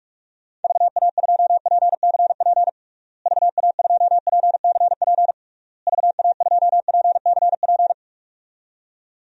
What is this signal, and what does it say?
Morse code, heard as a keyed tone, about 47 wpm, 700 Hz. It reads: VA2PCP VA2PCP VA2PCP
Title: DXCC-100 Call Signs - 6 Characters - Repeated 3x - 45wpm